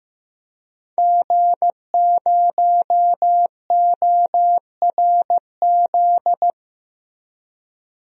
Morse code, heard as a keyed tone, about 15 wpm, 700 Hz. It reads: G0ORZ
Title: DXCC-100 Call Signs - 5 Characters - 15wpm